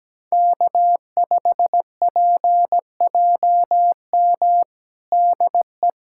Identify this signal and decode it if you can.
Morse code, heard as a keyed tone, about 17 wpm, 700 Hz. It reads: K5PJM DE
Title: USA Callsign DE Callsign - 5 Characters - 17wpm